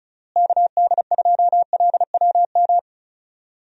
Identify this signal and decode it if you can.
Morse code, heard as a keyed tone, about 36 wpm, 700 Hz. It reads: KD2LWM
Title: USA Call Signs - 6 Characters - Repeated 3x - 35wpm